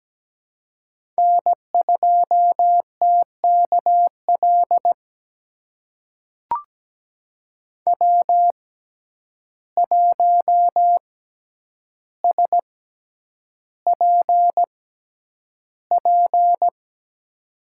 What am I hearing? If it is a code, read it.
Morse code, heard as a keyed tone, about 17 wpm, spaced as for 9 wpm, 700 Hz. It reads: N2TKL W1SPP
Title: USA Call Signs - 5 Characters - 6x Character Spacing - 17wpm